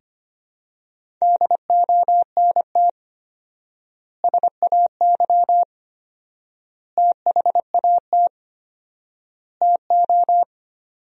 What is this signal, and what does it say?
Morse code, heard as a keyed tone, about 25 wpm, 700 Hz. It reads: DONT SAY THAT TO
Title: 4x Word-Spacing - Sentences from Top 300 Words - 25wpm